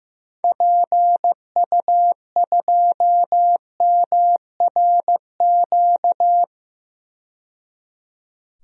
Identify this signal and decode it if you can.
Morse code, heard as a keyed tone, about 15 wpm, 700 Hz. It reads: PU2MRQ
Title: DXCC-100 Call Signs - 6 Characters - 15wpm